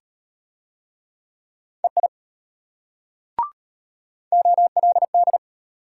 Morse code, heard as a keyed tone, about 34 wpm, 700 Hz. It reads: EI OLD